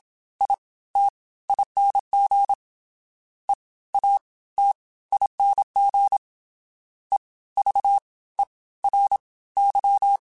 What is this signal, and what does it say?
Morse code, heard as a keyed tone, about 27 wpm, 800 Hz. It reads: I T ING E A T ING E V E R Y